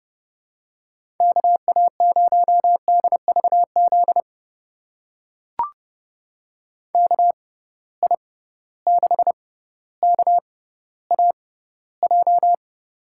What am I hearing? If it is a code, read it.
Morse code, heard as a keyed tone, about 32 wpm, spaced as for 16 wpm, 700 Hz. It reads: KA0DVZ KI6KAJ
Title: USA Call Signs - 6 Characters - 6x Character Spacing - 30wpm